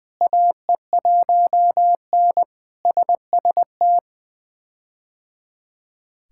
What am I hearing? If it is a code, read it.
Morse code, heard as a keyed tone, about 20 wpm, 700 Hz. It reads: AE1N SST
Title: K1USN SST QSO Practice - 20wpm